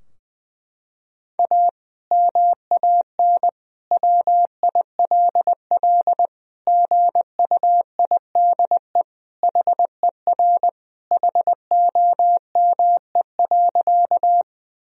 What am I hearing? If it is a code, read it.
Morse code, heard as a keyed tone, about 21 wpm, 700 Hz. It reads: A MAN WILL GUIDE HER HOME.
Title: Sentences from Top 1000 Words - Review Part 3 - 20wpm